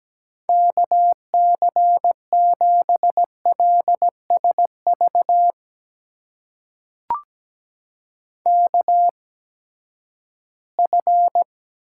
Morse code, heard as a keyed tone, about 17 wpm, spaced as for 12 wpm, 700 Hz. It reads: KC7LSV KF
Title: USA Call Signs - 6 Characters - 8x Character Spacing - 17wpm